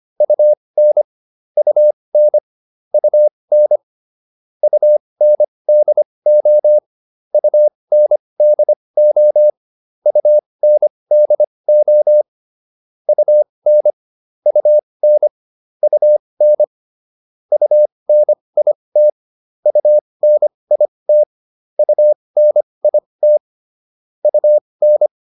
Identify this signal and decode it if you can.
Morse code, heard as a keyed tone, about 25 wpm, spaced as for 20 wpm, 600 Hz. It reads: UN UN UN UNDO UNDO UNDO UN UN UN UNIT UNIT UNIT UN